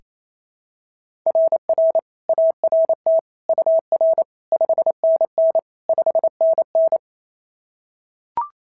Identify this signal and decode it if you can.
Morse code, heard as a keyed tone, about 25 wpm, 650 Hz. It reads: RR ART UR 5NN 5NN
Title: SOTA QSO Practice 28wpm